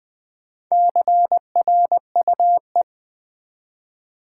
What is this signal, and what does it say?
Morse code, heard as a keyed tone, about 20 wpm, 700 Hz. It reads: CRUE
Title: Build A Word - 1 Letter At A Time - 3 to 5-Letter Words - 20wpm